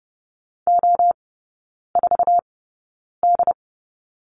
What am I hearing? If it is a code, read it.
Morse code, heard as a keyed tone, about 30 wpm, spaced as for 9 wpm, 700 Hz. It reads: O4D
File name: Lesson 16e - Letter c - Three Characters 30wpm